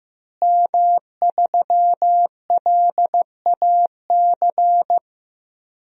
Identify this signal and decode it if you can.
Morse code, heard as a keyed tone, about 15 wpm, 700 Hz. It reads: M3LAC
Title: DXCC-100 Call Signs - 5 Characters - Repeated 3x - 15wpm